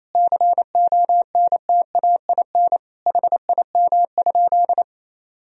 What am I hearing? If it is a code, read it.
Morse code, heard as a keyed tone, about 28 wpm, 700 Hz. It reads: CONTAIN HIM?
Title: Sentences from Top 1000 Words - Review Part 2 - 28wpm